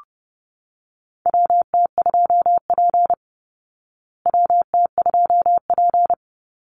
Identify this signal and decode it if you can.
Morse code, heard as a keyed tone, about 30 wpm, 700 Hz. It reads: WT2P WT2P
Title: Call Signs Repeated 2x - 30wpm